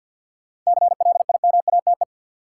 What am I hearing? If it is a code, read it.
Morse code, heard as a keyed tone, about 53 wpm, 700 Hz. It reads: CLIMATE
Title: Sentences from Top 1600 Words - New Word Drill Alt - 50wpm